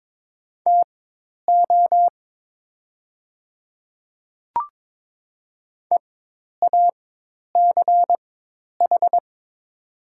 Morse code, heard as a keyed tone, about 22 wpm, spaced as for 10 wpm, 700 Hz. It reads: TO EACH
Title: 4x Character Spacing - Top 100 Words - 22wpm